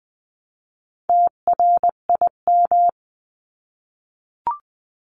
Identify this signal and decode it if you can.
Morse code, heard as a keyed tone, about 20 wpm, 700 Hz. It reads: TRIM
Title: 259 Lesson 42d - Words 18-20wpm